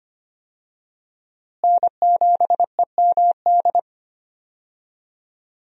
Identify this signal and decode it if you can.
Morse code, heard as a keyed tone, about 25 wpm, 700 Hz. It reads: N7EMD
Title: USA Call Signs - 5 Characters - Repeated 3x - 25wpm